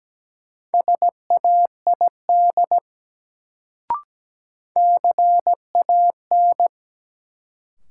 Morse code, heard as a keyed tone, about 17 wpm, 700 Hz. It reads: SAID CAN
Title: Sentences from Top 100 Words - New Word Drill - 17wpm